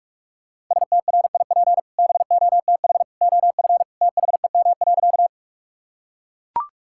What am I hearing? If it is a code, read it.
Morse code, heard as a keyed tone, about 42 wpm, 700 Hz. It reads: STRIP BOTH OF THEM.
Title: Sentences from Top 1600 Words 45wpm